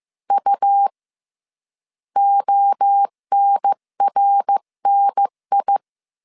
Morse code, heard as a keyed tone, about 15 wpm, 800 Hz. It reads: U ONRNI
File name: CW / DET / 1 3 1500C S A 14wpm 2447